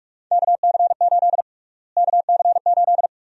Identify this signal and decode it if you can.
Morse code, heard as a keyed tone, about 41 wpm, 700 Hz. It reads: KC8 KC8
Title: USA Call Sign Prefixes - Repeated 2x - 45wpm